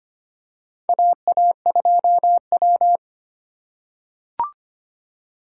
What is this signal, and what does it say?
Morse code, heard as a keyed tone, about 25 wpm, 700 Hz. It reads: AA2W
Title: USA Call Signs - 3x Character Spacing - 25wpm